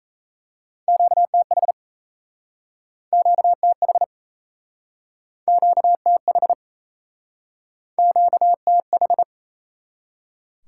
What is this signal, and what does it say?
Morse code, heard as a keyed tone, about 39 wpm, 700 Hz. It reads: QTH QTH QTH QTH